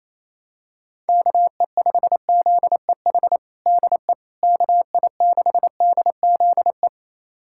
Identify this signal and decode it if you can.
Morse code, heard as a keyed tone, about 27 wpm, 700 Hz. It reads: KE5ZEH DE KI6DZE
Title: USA Callsign DE Callsign 28wpm